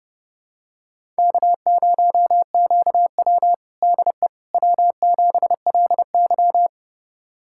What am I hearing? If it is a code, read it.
Morse code, heard as a keyed tone, about 32 wpm, 700 Hz. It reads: K0QW DE W7LY
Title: USA Callsign DE Callsign - 4 Characters - 30wpm